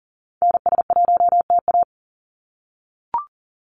Morse code, heard as a keyed tone, about 37 wpm, 700 Hz. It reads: NS1TA
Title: NA SOTA Activator Callsigns 40wpm